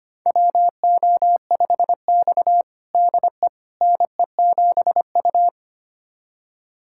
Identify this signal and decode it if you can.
Morse code, heard as a keyed tone, about 25 wpm, 700 Hz. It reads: WO5X DE NE7U
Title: USA Callsign DE Callsign - 4 Characters - 25wpm